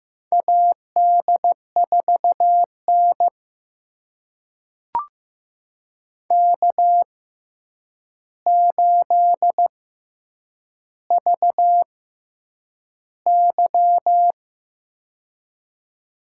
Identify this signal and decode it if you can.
Morse code, heard as a keyed tone, about 15 wpm, spaced as for 8 wpm, 700 Hz. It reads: AD4N K8VY
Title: USA Call Signs - 4 Characters - 6x Character Spacing - 15wpm